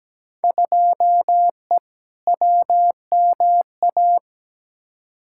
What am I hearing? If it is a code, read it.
Morse code, heard as a keyed tone, about 17 wpm, 700 Hz. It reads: 2E WMA
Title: Field Day Exchanges Encore - 17wpm